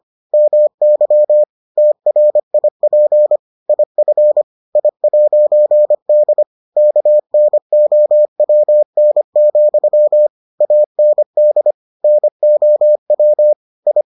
Morse code, heard as a keyed tone, about 25 wpm, 600 Hz. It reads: MY TRIP IF I'D KNOWN, AND NOW I